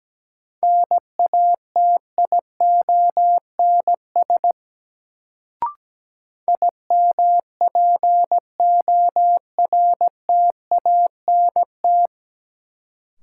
Morse code, heard as a keyed tone, about 17 wpm, 700 Hz. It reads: NATIONS IMPORTANT
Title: Sets of 3 Words - New Word Drill - 17wpm